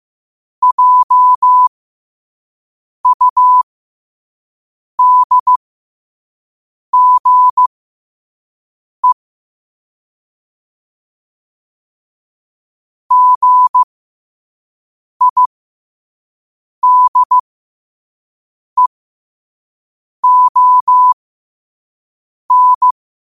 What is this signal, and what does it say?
Morse code, heard as a keyed tone, about 15 wpm, spaced as for 5 wpm, 1000 Hz. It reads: JUDGE GIDEON
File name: qotdfiles / 5wpm